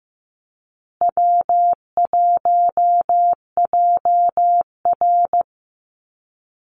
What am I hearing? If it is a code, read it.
Morse code, heard as a keyed tone, about 15 wpm, 700 Hz. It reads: W1JR
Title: Easy Call Signs Repeated 3x - 15wpm